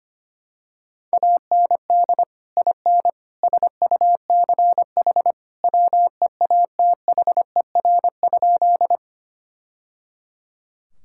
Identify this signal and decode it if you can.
Morse code, heard as a keyed tone, about 25 wpm, 700 Hz. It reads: AND IN SUCH WEATHER?
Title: Sentences from Top 1000 Words - Review Part 5 - 25wpm